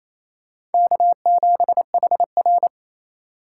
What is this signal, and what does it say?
Morse code, heard as a keyed tone, about 28 wpm, 700 Hz. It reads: K7HR
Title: USA Call Signs - 4 Characters - 4x Character Spacing - 28wpm